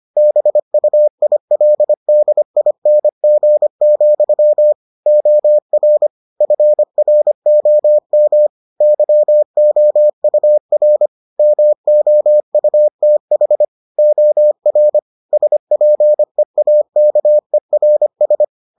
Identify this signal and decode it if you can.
Morse code, heard as a keyed tone, about 25 wpm, 600 Hz. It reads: BUILDING, OR FROM YOUR MOUTH OR SPEAKERS